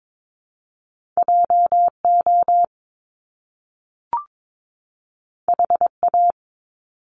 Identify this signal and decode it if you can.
Morse code, heard as a keyed tone, about 22 wpm, 700 Hz. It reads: JO HA
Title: Top Two Letter Combinations in Words 22wpm